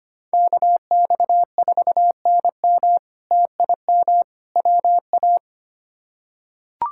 Morse code, heard as a keyed tone, about 25 wpm, 700 Hz. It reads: KX4NM TIM WA
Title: K1USN SST QSO Practice - 25wpm